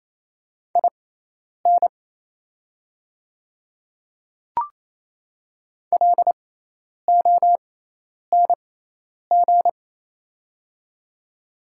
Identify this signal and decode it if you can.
Morse code, heard as a keyed tone, about 28 wpm, spaced as for 10 wpm, 700 Hz. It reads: IN LONG
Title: 6x Character Spacing - Top 100 Words - 28wpm